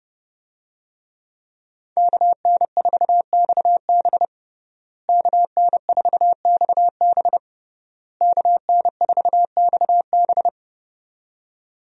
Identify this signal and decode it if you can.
Morse code, heard as a keyed tone, about 30 wpm, 700 Hz. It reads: KN4XB KN4XB KN4XB
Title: USA Call Signs - 5 Characters - Repeated 3x - 30wpm